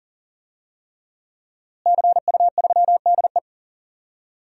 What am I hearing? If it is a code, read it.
Morse code, heard as a keyed tone, about 40 wpm, 700 Hz. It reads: CU3DE
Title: DXCC-100 Call Signs - 5 Characters - Repeated 3x - 40wpm